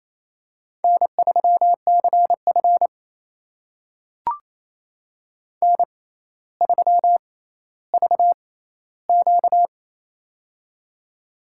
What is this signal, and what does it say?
Morse code, heard as a keyed tone, about 28 wpm, spaced as for 14 wpm, 700 Hz. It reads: N3CF N3VQ
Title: USA Call Signs - 4 Characters - 6x Character Spacing - 28wpm